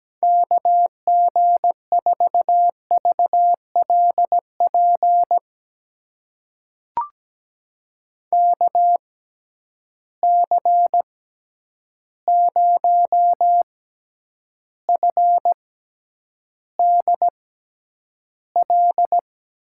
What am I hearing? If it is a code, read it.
Morse code, heard as a keyed tone, about 17 wpm, spaced as for 9 wpm, 700 Hz. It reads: KG4VLP KC0FDL
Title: USA Call Signs - 6 Characters - 6x Character Spacing - 17wpm